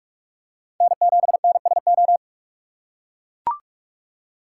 Morse code, heard as a keyed tone, about 44 wpm, 700 Hz. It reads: N7NSO
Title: USA Call Signs - 5 Characters - 6x Character Spacing - 45wpm